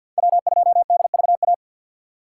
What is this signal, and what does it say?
Morse code, heard as a keyed tone, about 46 wpm, 700 Hz. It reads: W2DVA
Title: USA Call Signs - 5 Characters - 2x Character Spacing - 50wpm